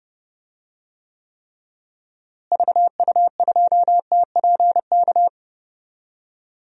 Morse code, heard as a keyed tone, about 31 wpm, 700 Hz. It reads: VU2TPK
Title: DXCC-100 Call Signs - 6 Characters - Repeated 3x - 30wpm